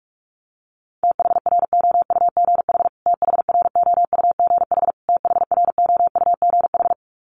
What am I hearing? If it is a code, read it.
Morse code, heard as a keyed tone, about 44 wpm, 700 Hz. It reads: THROUGH THROUGH THROUGH
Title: Top 500 Words Repeated 45wpm